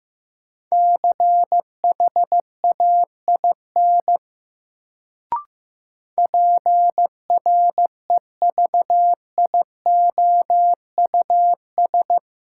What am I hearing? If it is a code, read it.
Morse code, heard as a keyed tone, about 15 wpm, 700 Hz. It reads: CHAIN PREVIOUS
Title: Sentences from Top 1500 Words - New Word Drill - 15wpm